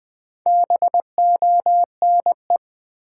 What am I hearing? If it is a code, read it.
Morse code, heard as a keyed tone, about 20 wpm, 700 Hz. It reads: BONE
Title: Build A Word - 2 Letters At A Time - 4-Letter Words - 20wpm